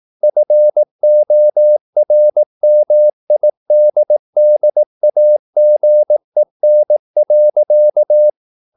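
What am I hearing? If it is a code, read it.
Morse code, heard as a keyed tone, about 18 wpm, 600 Hz. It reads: FORMIDDAGEN.
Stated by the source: Slow (QRS) Morse practice at 18 wpm, CW med Gnister.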